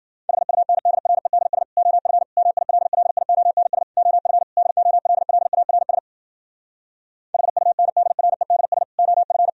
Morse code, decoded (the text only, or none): HUNDREDS OF MILLIONS OF DOLLARS HUNDREDS OF